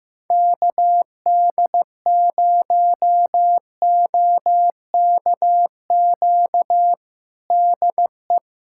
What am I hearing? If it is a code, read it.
Morse code, heard as a keyed tone, about 16 wpm, 700 Hz. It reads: KD0OKQ DE
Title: USA Callsign DE Callsign - 6 Characters - 15wpm